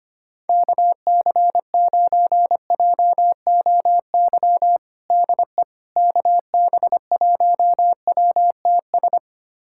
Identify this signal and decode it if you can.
Morse code, heard as a keyed tone, about 25 wpm, 700 Hz. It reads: KC9JOY DE KB1WTS
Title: USA Callsign DE Callsign - 6 Characters - 25wpm